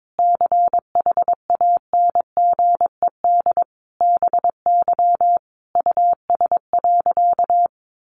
Morse code, heard as a keyed tone, about 22 wpm, 700 Hz. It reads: CHANGED BY US.
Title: A Sound Of Thunder 22wpm